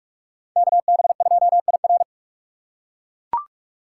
Morse code, heard as a keyed tone, about 41 wpm, 700 Hz. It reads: KB2IR